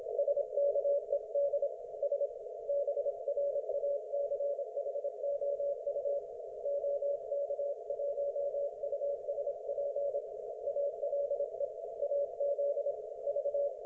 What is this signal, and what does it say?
Morse code, heard as a keyed tone, about 26 wpm, 570 Hz. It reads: SKED S BRACHOU OK1USP ALE AZ V
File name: Archiv / CUC216 / ok1fhi cul216